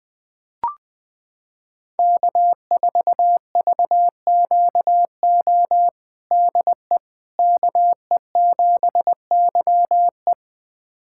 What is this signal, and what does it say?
Morse code, heard as a keyed tone, about 21 wpm, 700 Hz. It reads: K4VQO DE KE7YE
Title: USA Callsign DE Callsign - 5 Characters - 20wpm